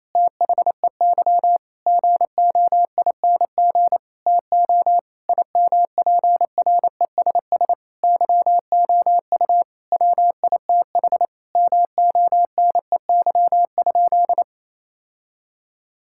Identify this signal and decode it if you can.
Morse code, heard as a keyed tone, about 28 wpm, 700 Hz. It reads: THEY GOING TO IMPRESS YOU WITH MONEY?
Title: Sentences from Top 1600 Words 28wpm